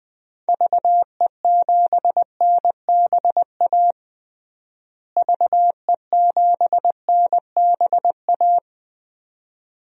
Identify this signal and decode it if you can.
Morse code, heard as a keyed tone, about 20 wpm, 700 Hz. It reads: VE7NBA VE7NBA
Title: DXCC-100 Call Signs - 6 Characters - Repeated 2x - 20wpm